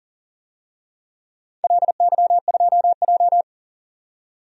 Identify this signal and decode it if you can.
Morse code, heard as a keyed tone, about 40 wpm, 700 Hz. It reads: LY2J
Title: DXCC-100 Call Signs - 4 Characters - 40wpm